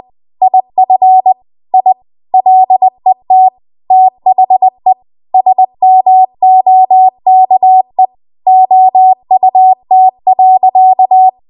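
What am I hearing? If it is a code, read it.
Morse code, heard as a keyed tone, about 20 wpm, 750 Hz. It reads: IF I LET THE SMOKE OUT.
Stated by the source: Morse code practice at 20 WPM.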